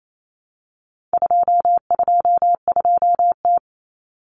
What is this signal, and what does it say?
Morse code, heard as a keyed tone, about 28 wpm, 700 Hz. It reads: 222T
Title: Common Cut Numbers - 4-Digit 28wpm